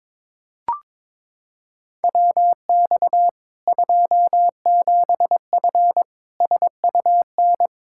Morse code, heard as a keyed tone, about 21 wpm, 700 Hz. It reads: WX 27F SUN
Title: Practice QSO - Exchange 2 - Info - 22wpm